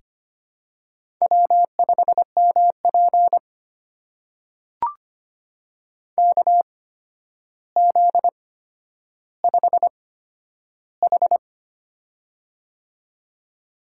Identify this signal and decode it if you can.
Morse code, heard as a keyed tone, about 25 wpm, spaced as for 11 wpm, 700 Hz. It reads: W5MP KZ5H
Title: USA Call Signs - 4 Characters - 8x Character Spacing - 25wpm